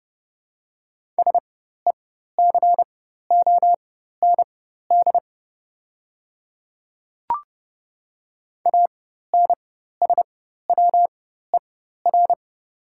Morse code, heard as a keyed tone, about 31 wpm, spaced as for 14 wpm, 700 Hz. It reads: SECOND ANSWER
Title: 4x Character Spacing - Top 300 Words - 30wpm